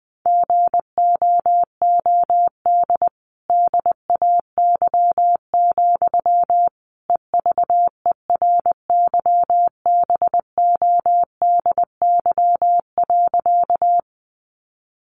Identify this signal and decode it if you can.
Morse code, heard as a keyed tone, about 20 wpm, 700 Hz. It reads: GOOD DAY, EVERYBODY.
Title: Sentences from Top 1400 Words 20wpm